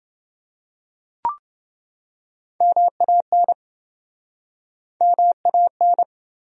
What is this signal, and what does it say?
Morse code, heard as a keyed tone, about 29 wpm, 700 Hz. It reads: MAN MAN